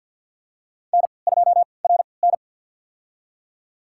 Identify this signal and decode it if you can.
Morse code, heard as a keyed tone, about 45 wpm, spaced as for 27 wpm, 700 Hz. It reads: N2RN
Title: USA Call Signs - 4 Characters - 3x Character Spacing - 50wpm